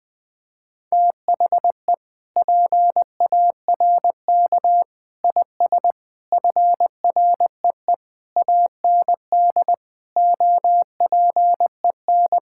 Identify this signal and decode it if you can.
Morse code, heard as a keyed tone, about 20 wpm, 700 Hz. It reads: THE PARK IS FREE AND OPEN
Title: Sentences with 7-Word Phrases - 20wpm